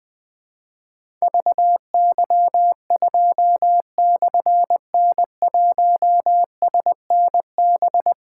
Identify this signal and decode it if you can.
Morse code, heard as a keyed tone, about 20 wpm, 700 Hz. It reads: VY2/N1SNB
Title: Difficult Call Signs 20wpm